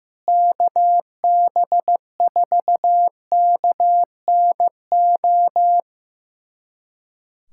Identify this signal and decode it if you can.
Morse code, heard as a keyed tone, about 15 wpm, 700 Hz. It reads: KB4KNO
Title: USA Callsign DE Callsign - 6 Characters - 15wpm